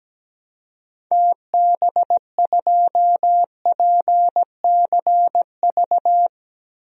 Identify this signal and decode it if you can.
Morse code, heard as a keyed tone, about 17 wpm, 700 Hz. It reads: TB2PCV
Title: DXCC-100 Call Signs - 6 Characters - 17wpm